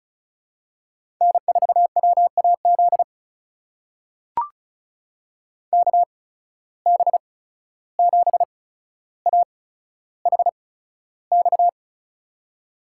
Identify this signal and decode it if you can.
Morse code, heard as a keyed tone, about 36 wpm, spaced as for 14 wpm, 700 Hz. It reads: N4WAZ KB7AHX
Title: USA Call Signs - 8x Character Spacing - 35wpm